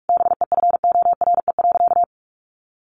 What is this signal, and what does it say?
Morse code, heard as a keyed tone, about 44 wpm, 700 Hz. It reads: BEFORE.
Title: Sentences from Top 1400 Words 45wpm